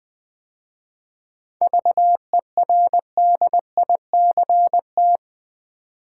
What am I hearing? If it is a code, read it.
Morse code, heard as a keyed tone, about 20 wpm, 700 Hz. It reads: VERDICT
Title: 7 Letter Words - First 3 Letters Given - 20wpm